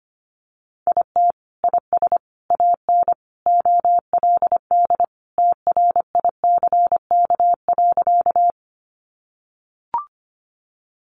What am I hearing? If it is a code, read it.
Morse code, heard as a keyed tone, about 25 wpm, 700 Hz. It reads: IT IS AN OLD TRICK.
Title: Sentences from Top 1500 Words 25wpm